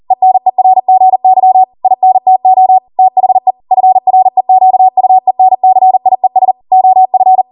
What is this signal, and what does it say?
Morse code, heard as a keyed tone, about 40 wpm, 750 Hz. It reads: ENERGY INTO THE FREQUENCIES OF